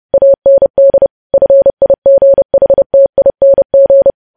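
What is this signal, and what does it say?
Morse code, heard as a keyed tone, about 30 wpm, 550 Hz. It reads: AND FIGHTING
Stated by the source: News headlines sent at 30 wpm.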